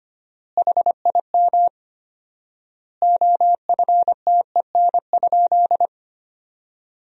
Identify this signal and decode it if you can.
Morse code, heard as a keyed tone, about 25 wpm, 700 Hz. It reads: HIM OFTEN?
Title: 4x Word-Spacing - Sentences from Top 400 Words - 25wpm